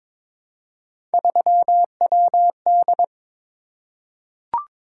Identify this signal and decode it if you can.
Morse code, heard as a keyed tone, about 22 wpm, 700 Hz. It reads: 3WD
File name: practice / 22wpm / Three Letter-Number Combinations - 22wpm